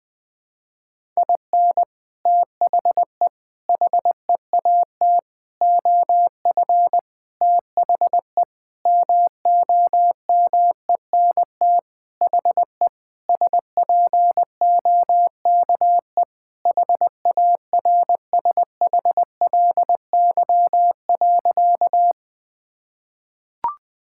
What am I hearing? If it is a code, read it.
Morse code, heard as a keyed tone, about 21 wpm, 700 Hz. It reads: IN THE HEAT OF THE MOMENT HE SPOKE HARSHLY.
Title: Sentences with 6-Word Phrases - 20wpm